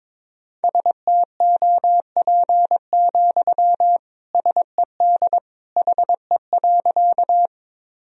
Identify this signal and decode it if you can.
Morse code, heard as a keyed tone, about 22 wpm, 700 Hz. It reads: STOP, SED HE.
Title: Sentences from Top 300 Words and Abbreviations 22wpm